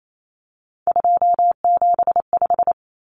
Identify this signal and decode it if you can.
Morse code, heard as a keyed tone, about 28 wpm, 700 Hz. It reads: 275
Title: Numbers - 3-Digit 28wpm